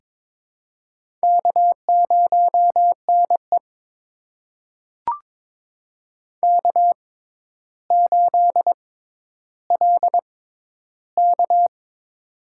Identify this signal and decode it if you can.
Morse code, heard as a keyed tone, about 22 wpm, spaced as for 11 wpm, 700 Hz. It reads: K0NE K8LK
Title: USA Call Signs - 4 Characters - 6x Character Spacing - 22wpm